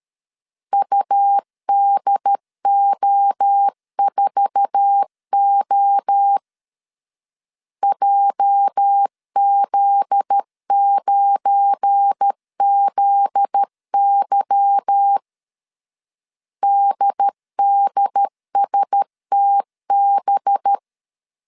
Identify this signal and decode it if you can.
Morse code, heard as a keyed tone, about 13 wpm, 800 Hz. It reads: UDO4O JZ9ZY DDSTB
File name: LEZIONI / 10 SIGNS / 10A SLASH 12wpm 2314